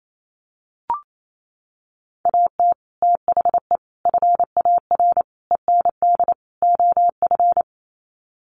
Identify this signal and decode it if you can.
Morse code, heard as a keyed tone, about 28 wpm, 700 Hz. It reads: AT THE FAR END OF
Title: Sets of 5 Words 28wpm